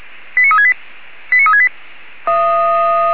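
Five-tone selective call (ZVEI), heard as quick runs of five tones, tones of about 70 ms. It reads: 89368 89368
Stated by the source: ZVEI five-tone call received by SDR.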